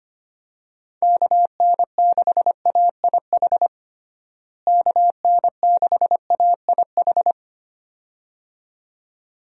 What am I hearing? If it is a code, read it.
Morse code, heard as a keyed tone, about 25 wpm, 700 Hz. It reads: KN6AIH KN6AIH
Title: USA Call Signs - 6 Characters - Repeated 2x - 25wpm